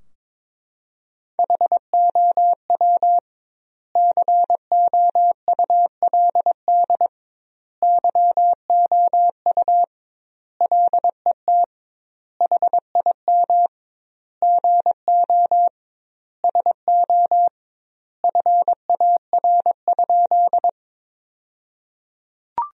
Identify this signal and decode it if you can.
Morse code, heard as a keyed tone, about 22 wpm, 700 Hz. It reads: HOW COULD YOU LET HIM GO SO FAR?
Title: 2x Word-Spacing - Sentences from Top 300 Words - 22wpm